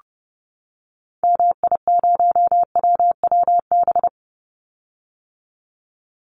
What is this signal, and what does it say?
Morse code, heard as a keyed tone, about 30 wpm, 700 Hz. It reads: MI0WWB
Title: Call Signs - 30wpm